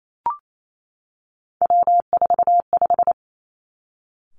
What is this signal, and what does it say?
Morse code, heard as a keyed tone, about 28 wpm, 700 Hz. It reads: W45
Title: Three Challenging Letter-Number Combinations 28wpm